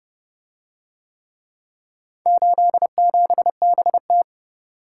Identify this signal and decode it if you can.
Morse code, heard as a keyed tone, about 29 wpm, 700 Hz. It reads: 87BT